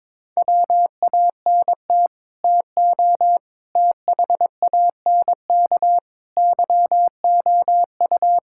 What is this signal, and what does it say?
Morse code, heard as a keyed tone, about 22 wpm, 700 Hz. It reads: WANT TO THANK YOU
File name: Sets of 9 Words - 22wpm